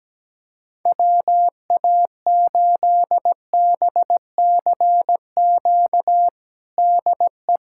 Morse code, heard as a keyed tone, about 17 wpm, 700 Hz. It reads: WA8BCQ DE
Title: USA Callsign DE Callsign - 6 Characters - 17wpm